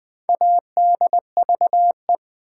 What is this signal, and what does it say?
Morse code, heard as a keyed tone, about 20 wpm, 700 Hz. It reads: ADVE
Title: Build A Word - 4 Letters At A Time - 11-Letter Words - 20wpm